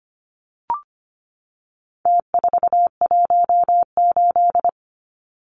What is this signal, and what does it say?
Morse code, heard as a keyed tone, about 25 wpm, 700 Hz. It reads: T418
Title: Numbers - Contest Style - 4-Digit 25wpm